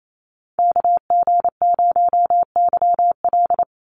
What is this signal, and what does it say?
Morse code, heard as a keyed tone, about 28 wpm, 700 Hz. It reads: KG0YL
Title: Call Signs Repeated 2x - 28wpm